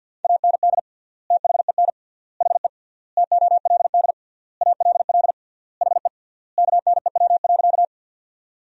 Morse code, heard as a keyed tone, about 52 wpm, 700 Hz. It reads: AND THEN HE TOLD ALL HE KNEW.
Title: 3x Word-Spacing - Sentences from Top 400 Words - 50wpm